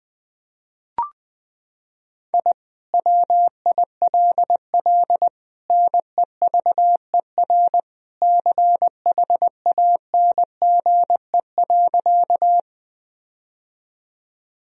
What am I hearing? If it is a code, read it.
Morse code, heard as a keyed tone, about 20 wpm, 700 Hz. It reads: I WILL NEVER CHANGE.
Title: Sentences from Top 1000 Words - Review Part 1 - 20wpm